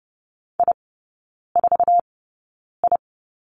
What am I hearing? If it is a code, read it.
Morse code, heard as a keyed tone, about 30 wpm, spaced as for 9 wpm, 700 Hz. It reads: I4I
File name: course / 30:10wpm / 087 Lesson 15e - Number 5 - Three Characters 30wpm